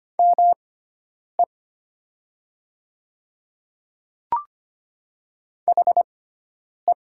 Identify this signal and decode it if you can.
Morse code, heard as a keyed tone, about 25 wpm, spaced as for 9 wpm, 690 Hz. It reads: ME HE
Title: 6x Character Spacing - Top 200 Words - 25wpm